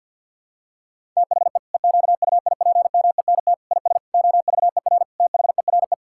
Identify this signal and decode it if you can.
Morse code, heard as a keyed tone, about 50 wpm, 700 Hz. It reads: THE EQUIPMENT IS OVER THERE
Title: Sentences from Top 1000 Words - Review Part 5 - 50wpm